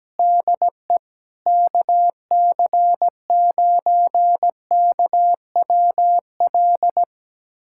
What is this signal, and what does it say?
Morse code, heard as a keyed tone, about 17 wpm, 700 Hz. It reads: DE KC9KWL
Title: USA Callsign DE Callsign - 6 Characters - 17wpm